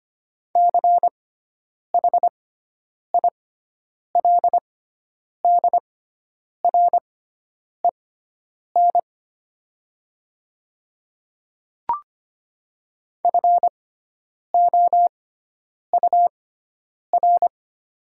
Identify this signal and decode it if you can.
Morse code, heard as a keyed tone, about 25 wpm, spaced as for 9 wpm, 700 Hz. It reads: CHILDREN FOUR
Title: 6x Character Spacing - Top 500 Words - 25wpm